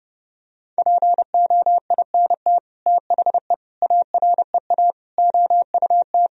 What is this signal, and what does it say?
Morse code, heard as a keyed tone, about 30 wpm, 700 Hz. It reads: POINT THE AREA OUT
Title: Sentences from Top 1000 Words - Review Part 2 - 30wpm